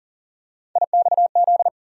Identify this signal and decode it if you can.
Morse code, heard as a keyed tone, about 40 wpm, 700 Hz. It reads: IXZ
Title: USA Call Sign Postfixes - 3 Letters 40wpm